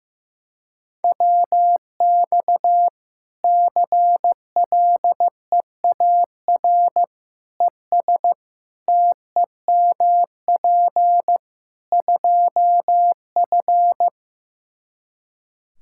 Morse code, heard as a keyed tone, about 15 wpm, 700 Hz. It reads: WX CLEAR ES TEMP 2F
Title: Practice QSO - Exchange 2 - Info - 15wpm